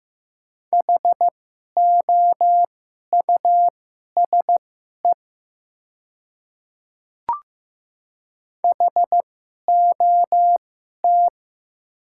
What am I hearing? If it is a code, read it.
Morse code, heard as a keyed tone, about 15 wpm, spaced as for 11 wpm, 700 Hz. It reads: HOUSE HOT
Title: 2x Character Spacing - Top 200 Words - 15wpm